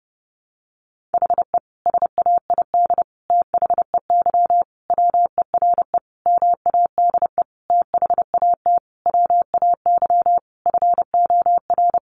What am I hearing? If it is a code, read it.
Morse code, heard as a keyed tone, about 30 wpm, 700 Hz. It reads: HE SAID THEY WERE MADE THAT WAY FOR
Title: Sentences from Top 700 Words 30wpm